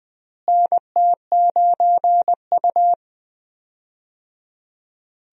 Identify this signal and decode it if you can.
Morse code, heard as a keyed tone, about 20 wpm, 700 Hz. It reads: NT9U
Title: USA Call Signs - 4 Characters - Repeated 3x - 20wpm